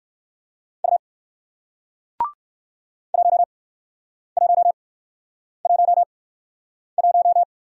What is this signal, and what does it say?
Morse code, heard as a keyed tone, about 58 wpm, 700 Hz. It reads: U 1 1 1 1